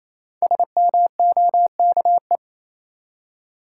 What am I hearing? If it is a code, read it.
Morse code, heard as a keyed tone, about 28 wpm, 700 Hz. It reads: SMOKE
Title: Five Letter Words - 28wpm